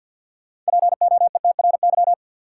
Morse code, heard as a keyed tone, about 50 wpm, 700 Hz. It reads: POETRY
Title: Six Letter Words - 50wpm